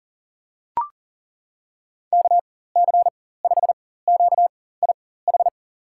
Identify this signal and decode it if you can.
Morse code, heard as a keyed tone, about 40 wpm, spaced as for 19 wpm, 700 Hz. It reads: KC5QIH
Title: USA Call Signs - 4x Character Spacing - 40wpm